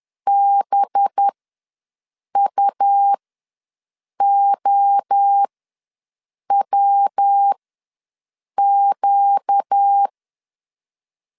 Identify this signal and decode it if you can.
Morse code, heard as a keyed tone, about 10 wpm, spaced as for 6 wpm, 800 Hz. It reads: BUOWQ
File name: LEZIONI / 5 LBVC / 5 LB ws 10wpm 3616